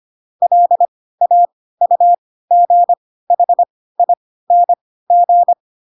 Morse code, heard as a keyed tone, about 25 wpm, spaced as for 16 wpm, 700 Hz. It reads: LAUGHING
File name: Pinocchio / Pinocchio ch1 10 25 ik1hge